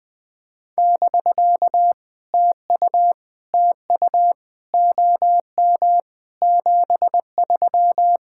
Morse code, heard as a keyed tone, about 20 wpm, 700 Hz. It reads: <BK> TU TU OM 73
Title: POTA QSO Practice - 20wpm